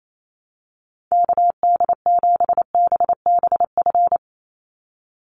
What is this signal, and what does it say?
Morse code, heard as a keyed tone, about 28 wpm, 700 Hz. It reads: KD7BBF
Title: USA Call Signs 28wpm